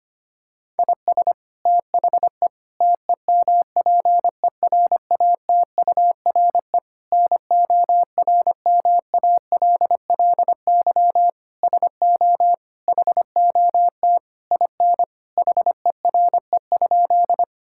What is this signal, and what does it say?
Morse code, heard as a keyed tone, about 25 wpm, 700 Hz. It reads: IS THE TEMPERATURE NORMALLY SO HOT IN HERE?
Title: Sentences from Top 1000 Words - Review Part 6 - 25wpm